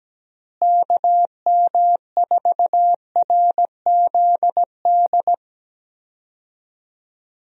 Morse code, heard as a keyed tone, about 17 wpm, 700 Hz. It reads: KM4RZD
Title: USA Call Signs - 6 Characters - Repeated 2x - 17wpm